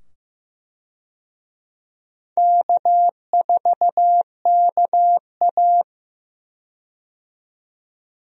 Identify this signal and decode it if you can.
Morse code, heard as a keyed tone, about 15 wpm, 700 Hz. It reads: K4KA
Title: USA Call Signs - 4 Characters - 15wpm